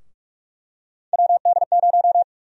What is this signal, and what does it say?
Morse code, heard as a keyed tone, about 46 wpm, 700 Hz. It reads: WD0
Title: USA Call Sign Prefixes - Repeated 2x - 45wpm